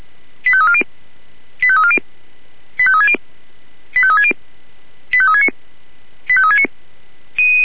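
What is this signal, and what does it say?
Five-tone selective call (ZVEI), heard as quick runs of five tones, tones of about 70 ms. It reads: 85340 85340 85360 85360 85369 85369
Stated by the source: ZVEI five-tone call received by SDR.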